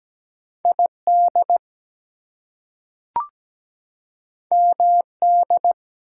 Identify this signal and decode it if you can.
Morse code, heard as a keyed tone, about 17 wpm, 700 Hz. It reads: ID MD
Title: US States and Canadian Provinces 17wpm